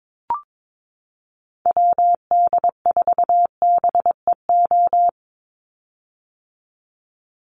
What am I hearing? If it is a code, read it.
Morse code, heard as a keyed tone, about 22 wpm, 700 Hz. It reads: WD4BEO
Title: USA Call Signs 22wpm